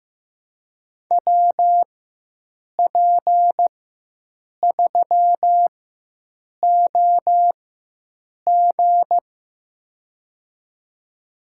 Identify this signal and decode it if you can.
Morse code, heard as a keyed tone, about 15 wpm, spaced as for 7 wpm, 700 Hz. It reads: WP3OG
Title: USA Call Signs - 5 Characters - 4x Character Spacing - 15wpm